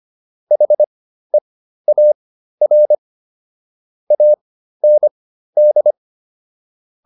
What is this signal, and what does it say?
Morse code, heard as a keyed tone, about 25 wpm, spaced as for 13 wpm, 600 Hz. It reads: HEAR AND